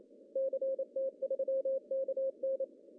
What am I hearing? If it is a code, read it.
Morse code, heard as a keyed tone, about 27 wpm, 540 Hz. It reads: CT3KN
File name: CW / cqww-cw-k5tr